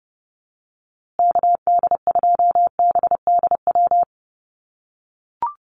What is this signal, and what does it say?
Morse code, heard as a keyed tone, about 30 wpm, 700 Hz. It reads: KD2BDW
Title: USA Call Signs Repeated 3x 30wpm